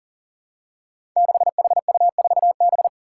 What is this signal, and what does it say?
Morse code, heard as a keyed tone, about 40 wpm, 700 Hz. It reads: 6HU4B
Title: Five Challenging Letter-Number Combinations 40wpm